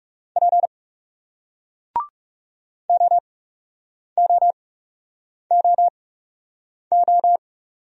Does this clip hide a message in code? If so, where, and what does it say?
Yes: Morse code from 0.4 to 7.4 s, heard as a keyed tone, about 39 wpm, spaced as for 8 wpm, 700 Hz. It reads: P OOOO